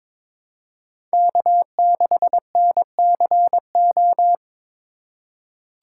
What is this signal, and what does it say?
Morse code, heard as a keyed tone, about 22 wpm, 700 Hz. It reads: K6NCO
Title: USA Call Signs - 5 Characters - 3x Character Spacing - 22wpm